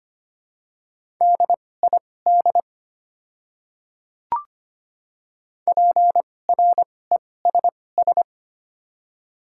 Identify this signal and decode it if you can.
Morse code, heard as a keyed tone, about 25 wpm, spaced as for 18 wpm, 700 Hz. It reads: DID PRESS
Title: 2x Character Spacing - Top 300 Words - 25wpm